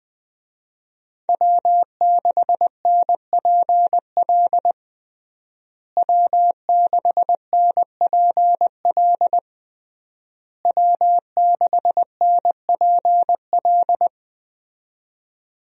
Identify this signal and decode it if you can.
Morse code, heard as a keyed tone, about 20 wpm, 700 Hz. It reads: W6NPL W6NPL W6NPL
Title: USA Call Signs - 5 Characters - Repeated 3x - 20wpm